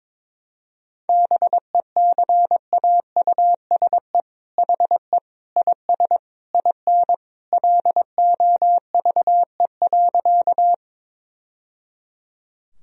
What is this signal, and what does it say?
Morse code, heard as a keyed tone, about 22 wpm, 700 Hz. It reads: BECAUSE HE IS IN LOVE.
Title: Sentences from Top 1000 Words - Review Part 3 - 22wpm